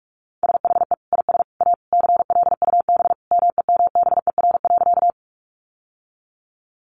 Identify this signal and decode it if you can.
Morse code, heard as a keyed tone, about 44 wpm, 700 Hz. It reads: SHE IS A CLUB MEMBER.
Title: Sentences from Top 1200 Words 45wpm